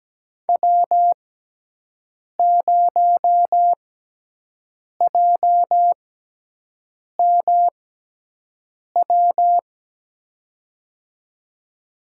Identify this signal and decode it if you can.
Morse code, heard as a keyed tone, about 17 wpm, spaced as for 6 wpm, 700 Hz. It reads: W0JMW
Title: USA Call Signs - 6x Character Spacing - 17wpm